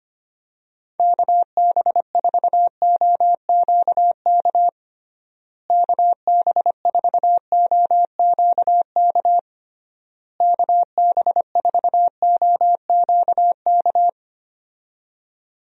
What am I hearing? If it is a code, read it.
Morse code, heard as a keyed tone, about 25 wpm, 700 Hz. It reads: KB4OQK KB4OQK KB4OQK
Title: USA Call Signs - 6 Characters - Repeated 3x - 25wpm